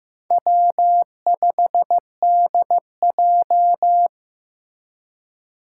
Morse code, heard as a keyed tone, about 15 wpm, 700 Hz. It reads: W5DJ
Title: Four Challenging Letter-Number Combinations 15wpm